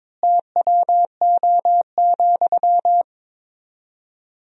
Morse code, heard as a keyed tone, about 22 wpm, 700 Hz. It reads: TWO,
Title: 4x Word-Spacing - Sentences from Top 200 Words - 22wpm